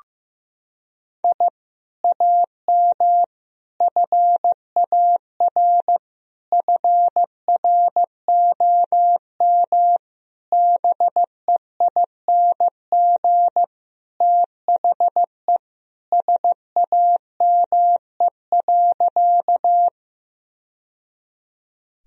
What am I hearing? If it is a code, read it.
Morse code, heard as a keyed tone, about 15 wpm, 700 Hz. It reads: I AM FAR FROM BEING THE SAME.
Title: Sentences from Top 1000 Words - Review Part 3 - 15wpm